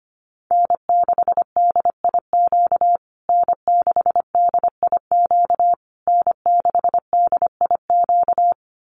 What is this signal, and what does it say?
Morse code, heard as a keyed tone, about 25 wpm, 700 Hz. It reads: N6DIQ N6DIQ N6DIQ
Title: USA Call Signs Repeated 3x 25wpm